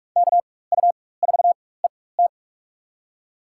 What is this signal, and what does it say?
Morse code, heard as a keyed tone, about 46 wpm, spaced as for 21 wpm, 700 Hz. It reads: KU4ET
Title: USA Call Signs - 5 Characters - 4x Character Spacing - 45wpm